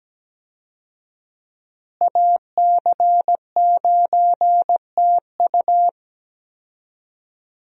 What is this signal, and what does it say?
Morse code, heard as a keyed tone, about 17 wpm, 700 Hz. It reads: AC9TU
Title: USA Call Signs - 5 Characters - Repeated 2x - 17wpm